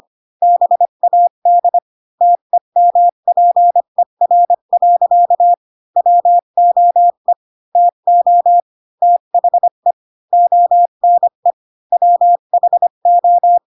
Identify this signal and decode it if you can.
Morse code, heard as a keyed tone, about 25 wpm, 700 Hz. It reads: BAD TEMPER. WOE TO THE ONE WHO